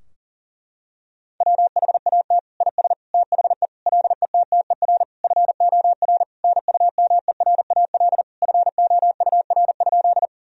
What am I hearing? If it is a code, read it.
Morse code, heard as a keyed tone, about 40 wpm, 700 Hz. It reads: WHAT IS THE LETTER FOR NUMERAL FOUR?
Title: Sentences from Top 1000 Words - Review Part 2 - 40wpm